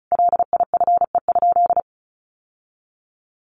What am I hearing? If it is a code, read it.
Morse code, heard as a keyed tone, about 34 wpm, 700 Hz. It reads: LIFE?
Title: Sentences from Top 1200 Words 35wpm